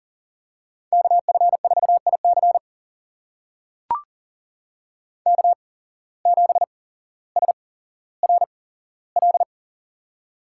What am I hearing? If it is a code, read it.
Morse code, heard as a keyed tone, about 40 wpm, spaced as for 17 wpm, 700 Hz. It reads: KF4IC K7SRL
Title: USA Call Signs - 5 Characters - 8x Character Spacing - 40wpm